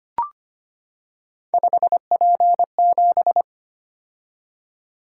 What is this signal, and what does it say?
Morse code, heard as a keyed tone, about 25 wpm, 700 Hz. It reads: 5P7
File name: Three Letter-Number Combinations - 25wpm